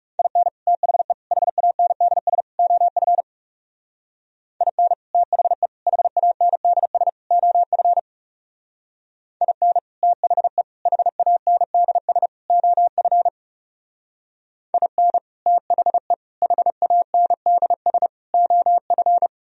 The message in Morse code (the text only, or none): IN THE HANDS OF IN THE HANDS OF IN THE HANDS OF IN THE HANDS OF